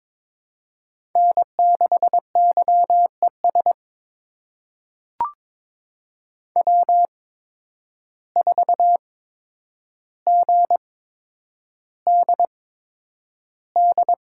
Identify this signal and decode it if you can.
Morse code, heard as a keyed tone, about 22 wpm, spaced as for 9 wpm, 700 Hz. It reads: N6YES W4GDD
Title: USA Call Signs - 5 Characters - 8x Character Spacing - 22wpm